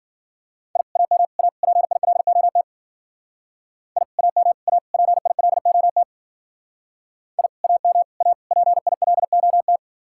Morse code, heard as a keyed tone, about 50 wpm, 700 Hz. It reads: I AM A PILOT I AM A PILOT I AM A PILOT